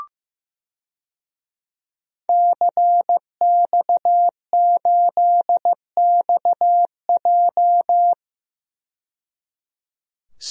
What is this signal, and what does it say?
Morse code, heard as a keyed tone, about 15 wpm, 700 Hz. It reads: CX8XJ
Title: DXCC-100 Call Signs - 5 Characters - 15wpm